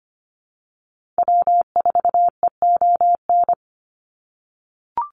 Morse code, heard as a keyed tone, about 25 wpm, 700 Hz. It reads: W4EON
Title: NA SOTA Activator Callsigns 25wpm